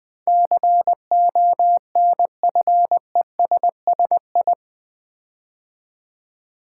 Morse code, heard as a keyed tone, about 20 wpm, 700 Hz. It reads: CONFESSI
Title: Build A Word - 4 Letters At A Time - 10-Letter Words - 20wpm